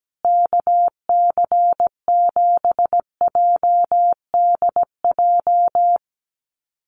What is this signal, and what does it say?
Morse code, heard as a keyed tone, about 17 wpm, 700 Hz. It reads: KC7JDJ
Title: USA Call Signs Repeated 2x 17wpm